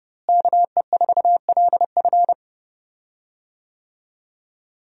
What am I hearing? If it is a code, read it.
Morse code, heard as a keyed tone, about 30 wpm, 700 Hz. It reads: KE4LF
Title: USA Call Signs - 5 Characters - Repeated 2x - 30wpm